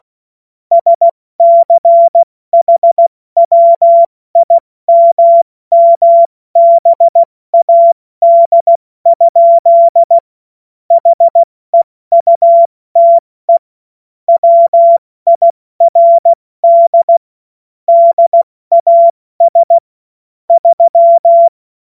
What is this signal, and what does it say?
Morse code, heard as a keyed tone, about 16 wpm, 680 Hz. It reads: SCHWIMMBAD? HEUTE WIRD DAS 3
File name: Morsekurs / Witze / Witz 21. September 2021, 70 F80